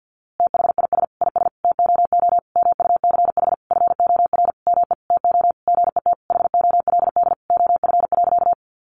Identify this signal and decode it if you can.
Morse code, heard as a keyed tone, about 50 wpm, 700 Hz. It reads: THIS IS TOO MUCH FOR ME TO GET HOLD OF.
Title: A Sound Of Thunder 50wpm